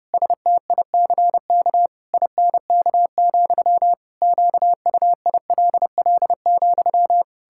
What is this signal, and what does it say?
Morse code, heard as a keyed tone, about 30 wpm, 700 Hz. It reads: STICK INK, QUILL,
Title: Pangram Challenge v2 - 30wpm